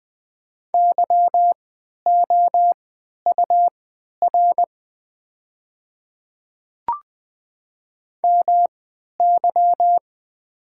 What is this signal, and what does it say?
Morse code, heard as a keyed tone, about 20 wpm, spaced as for 11 wpm, 700 Hz. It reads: YOUR MY
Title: 3x Character Spacing - Top 100 Words - 20wpm